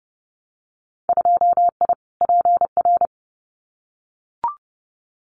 Morse code, heard as a keyed tone, about 30 wpm, 700 Hz. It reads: 2I PR
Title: Winter Field Day Exchanges Encore 30wpm